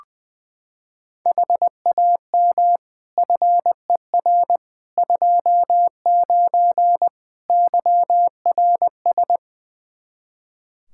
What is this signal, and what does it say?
Morse code, heard as a keyed tone, about 21 wpm, 700 Hz. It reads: HAM FER 29 YRS
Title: K1USN SST QSO Practice 20wpm